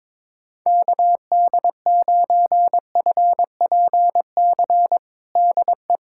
Morse code, heard as a keyed tone, about 22 wpm, 700 Hz. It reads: KD9FPC DE
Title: USA Callsign DE Callsign - 6 Characters - 22wpm